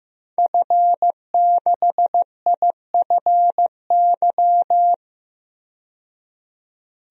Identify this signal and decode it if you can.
Morse code, heard as a keyed tone, about 15 wpm, 700 Hz. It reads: F6IFY
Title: DXCC-100 Call Signs - 5 Characters - 15wpm